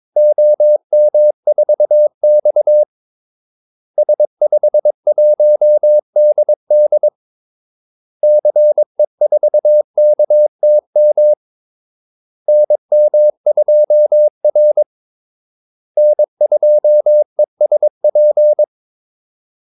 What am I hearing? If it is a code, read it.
Morse code, heard as a keyed tone, about 22 wpm, 600 Hz. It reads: OM4X S51DD CE4KTM NM2R N2ESP
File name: Callsigns-22wpm